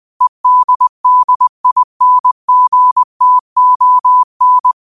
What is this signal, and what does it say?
Morse code, heard as a keyed tone, about 20 wpm, 1000 Hz. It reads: EDDINGTON
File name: qotdfiles / 20wpm